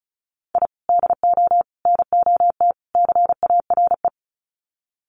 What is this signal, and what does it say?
Morse code, heard as a keyed tone, about 34 wpm, 700 Hz. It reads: I DO NOT CARE
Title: Sets of 4 Words 35wpm